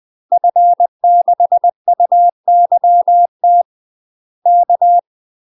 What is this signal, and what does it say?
Morse code, heard as a keyed tone, about 20 wpm, 700 Hz. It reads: F6UYT K
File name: UK MASTER QSOs TRAINING / 20WPM / UK MASTER QSOs 0014